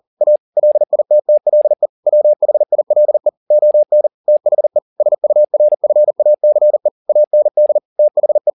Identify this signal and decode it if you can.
Morse code, heard as a keyed tone, about 40 wpm, 600 Hz. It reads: A LITTLE WHILE ON THE SURFACE AND THE
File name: the old man and the sea 2 40wpm